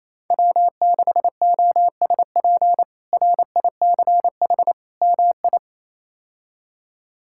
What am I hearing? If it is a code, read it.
Morse code, heard as a keyed tone, about 27 wpm, 700 Hz. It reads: W6OSP RICH MI
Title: K1USN SST QSO Practice - 28wpm